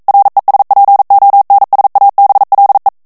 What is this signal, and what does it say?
Morse code, heard as a keyed tone, about 42 wpm, 800 Hz. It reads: RESPONSABLE
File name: morse mp3 / ESPAGNOL TEXTE EN CLAIR 1 VITESSE 40 MOTS MINUTES